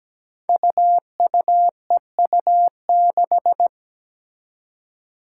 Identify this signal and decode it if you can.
Morse code, heard as a keyed tone, about 17 wpm, 700 Hz. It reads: UUEU6
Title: Five Challenging Letter-Number Combinations 17wpm